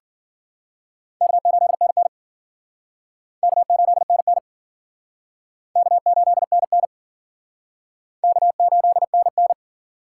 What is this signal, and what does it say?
Morse code, heard as a keyed tone, about 50 wpm, 700 Hz. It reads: K8NN K8NN K8NN K8NN
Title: Speed Racing - USA 1x2 Call Signs 40wpm